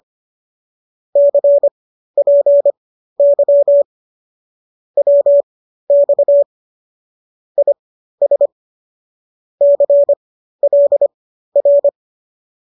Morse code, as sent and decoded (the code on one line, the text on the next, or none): -.-. .--. -.-- / .-- -..- / .. ... / -.-. .-.. .-.
CPY WX IS CLR